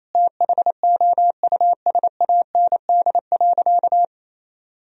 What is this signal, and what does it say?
Morse code, heard as a keyed tone, about 28 wpm, 700 Hz. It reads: THOUSAND.
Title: Sentences from Top 1000 Words - Review Part 3 - 28wpm